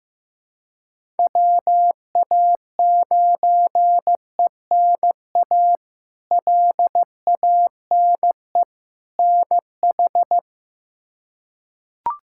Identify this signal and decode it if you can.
Morse code, heard as a keyed tone, about 15 wpm, 700 Hz. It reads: WA9ENA LANE NH
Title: K1USN SST QSO Practice - 15wpm